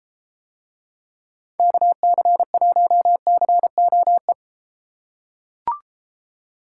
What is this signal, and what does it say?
Morse code, heard as a keyed tone, about 33 wpm, 700 Hz. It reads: KC1COE